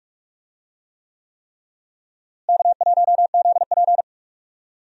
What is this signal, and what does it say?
Morse code, heard as a keyed tone, about 43 wpm, 700 Hz. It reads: K1ZP
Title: USA Call Signs - 4 Characters - Repeated 3x - 45wpm